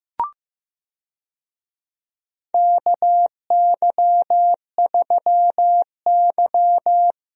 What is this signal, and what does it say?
Morse code, heard as a keyed tone, about 15 wpm, 700 Hz. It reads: KY3Y
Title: USA Call Signs - 4 Characters - Repeated 2x - 15wpm